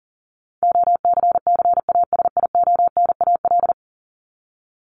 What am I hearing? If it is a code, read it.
Morse code, heard as a keyed tone, about 40 wpm, 700 Hz. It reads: OCCASIONAL
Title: LY Suffix - Top 100 Words with Base Words 40wpm